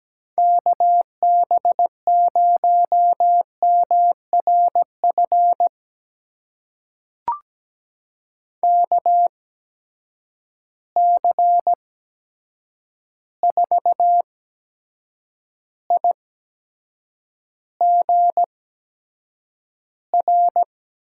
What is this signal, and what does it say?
Morse code, heard as a keyed tone, about 17 wpm, spaced as for 7 wpm, 700 Hz. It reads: KB0MRF KC4IGR
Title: USA Call Signs - 6 Characters - 8x Character Spacing - 17wpm